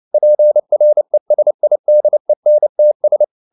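Morse code, heard as a keyed tone, about 29 wpm, 600 Hz. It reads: PRESIDENTS
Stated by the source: Morse code at 29 wpm, sending Ken Follett's novel Never in English, from CW med Gnister.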